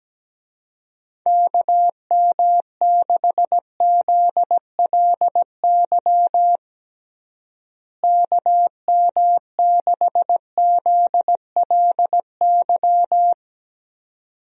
Morse code, heard as a keyed tone, about 17 wpm, 700 Hz. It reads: KM6ZLY KM6ZLY
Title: USA Call Signs - 6 Characters - Repeated 2x - 17wpm